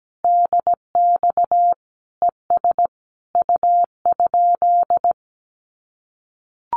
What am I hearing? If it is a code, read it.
Morse code, heard as a keyed tone, about 17 wpm, 700 Hz. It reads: DX ES U?
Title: Common QSO Phrases Encore 17wpm